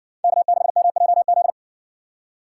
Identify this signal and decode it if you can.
Morse code, heard as a keyed tone, about 57 wpm, 700 Hz. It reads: K6MJZ